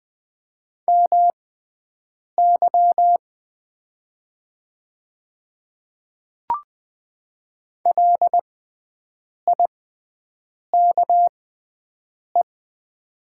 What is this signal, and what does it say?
Morse code, heard as a keyed tone, about 20 wpm, spaced as for 7 wpm, 700 Hz. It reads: MY LIKE
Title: 6x Character Spacing - Top 100 Words - 20wpm